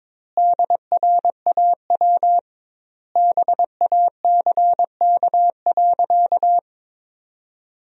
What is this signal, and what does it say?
Morse code, heard as a keyed tone, about 22 wpm, 700 Hz. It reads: DRAW BACK.
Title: 2x Word-Spacing - Sentences from Top 300 Words - 22wpm